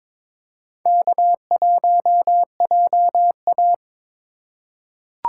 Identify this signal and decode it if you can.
Morse code, heard as a keyed tone, about 22 wpm, 700 Hz. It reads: K1JA
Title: USA Call Signs - 4 Characters - 2x Character Spacing - 22wpm